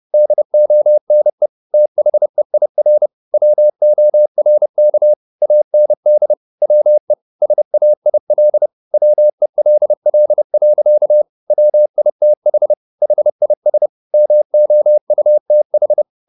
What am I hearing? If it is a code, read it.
Morse code, heard as a keyed tone, about 30 wpm, 600 Hz. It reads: DONE THEIR WORK AND WE SAIL WELL. WITH HIS MOUTH